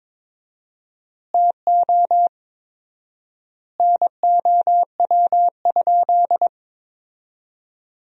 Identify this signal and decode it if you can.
Morse code, heard as a keyed tone, about 22 wpm, 700 Hz. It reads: TO NOW?
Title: 4x Word-Spacing - Sentences from Top 100 Words - 22wpm